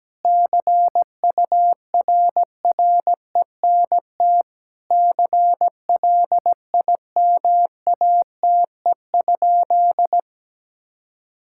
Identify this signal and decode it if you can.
Morse code, heard as a keyed tone, about 17 wpm, 700 Hz. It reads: CURRENT CLIMATE?
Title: Sentences from Top 1600 Words 17wpm